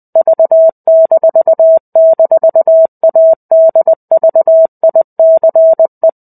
Morse code, heard as a keyed tone, about 20 wpm, 650 Hz. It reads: V--ADVICE
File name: 05-Chapter-05 20 WPM